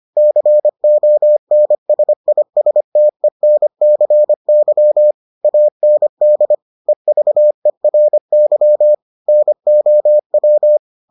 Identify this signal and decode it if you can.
Morse code, heard as a keyed tone, about 25 wpm, 600 Hz. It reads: CONSISTENCY AND EVERY NOW